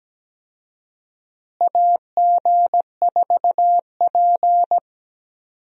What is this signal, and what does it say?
Morse code, heard as a keyed tone, about 17 wpm, 700 Hz. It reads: AG4P
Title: USA Call Signs - 4 Characters - Repeated 3x - 17wpm